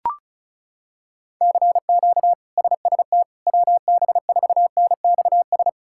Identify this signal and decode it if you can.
Morse code, heard as a keyed tone, about 36 wpm, 700 Hz. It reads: CQ SST WB4NXS
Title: K1USN SST QSO Practice - 35wpm